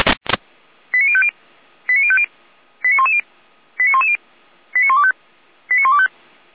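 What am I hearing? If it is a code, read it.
Five-tone selective call (ZVEI), heard as quick runs of five tones, tones of about 70 ms. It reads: 89050 89050 89110 89110 89125 89125
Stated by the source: ZVEI five-tone selective call received on an SDR.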